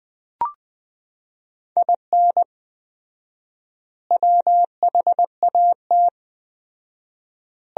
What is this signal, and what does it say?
Morse code, heard as a keyed tone, about 21 wpm, 700 Hz. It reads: IN WHAT
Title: 4x Word-Spacing - Sentences from Top 200 Words - 20wpm